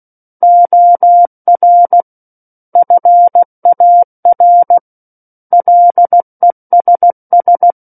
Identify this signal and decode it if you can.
Morse code, heard as a keyed tone, about 16 wpm, 700 Hz. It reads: OR FAR LESS
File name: Book One / 07-Chapter 7 - 16 WPM